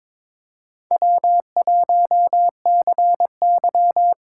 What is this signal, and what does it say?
Morse code, heard as a keyed tone, about 21 wpm, 700 Hz. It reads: W1CY
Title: USA Call Signs - 4 Characters - 8x Character Spacing - 22wpm